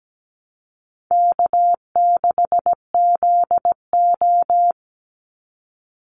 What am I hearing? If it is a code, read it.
Morse code, heard as a keyed tone, about 17 wpm, 700 Hz. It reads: K6ZO
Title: Easy Call Signs Repeated 2x - 17wpm